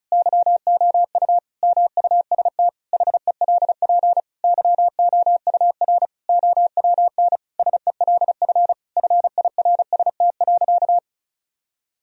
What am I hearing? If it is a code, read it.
Morse code, heard as a keyed tone, about 36 wpm, 700 Hz. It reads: YOU MUST HELP YOUR OWN SELF FIRST.
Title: Sentences from Top 1000 Words - Review Part 1 - 35wpm